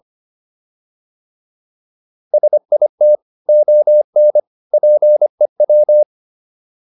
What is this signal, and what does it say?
Morse code, heard as a keyed tone, about 25 wpm, 600 Hz. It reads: SIT ON PEW